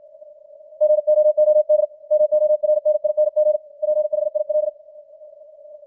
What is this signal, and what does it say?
Morse code, heard as a keyed tone, about 56 wpm, 610 Hz. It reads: GOOD MORNING PHIL
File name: archived / CWBros